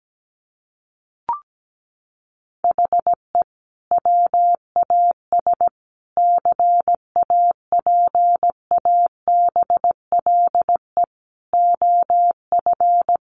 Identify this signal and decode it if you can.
Morse code, heard as a keyed tone, about 17 wpm, 700 Hz. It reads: HE WAS CAPABLE OF
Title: Sentences from Top 1200 Words 17wpm